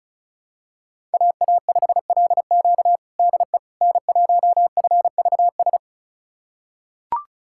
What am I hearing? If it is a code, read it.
Morse code, heard as a keyed tone, about 36 wpm, 700 Hz. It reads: AA5LQ DE N1FVS
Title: USA Callsign DE Callsign - 5 Characters - 35wpm